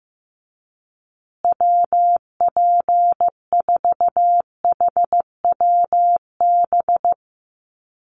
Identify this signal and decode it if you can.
Morse code, heard as a keyed tone, about 15 wpm, 700 Hz. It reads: WP4HWB
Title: USA Call Signs Repeated 2x 15wpm